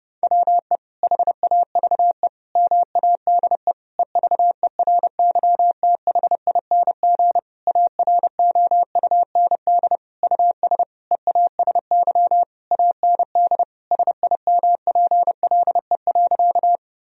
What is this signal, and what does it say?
Morse code, heard as a keyed tone, about 30 wpm, 700 Hz. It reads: WE HAVE MADE EVERYTHING AROUND US EASY AND SIMPLE.
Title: Sentences from Top 1000 Words - Review Part 3 - 30wpm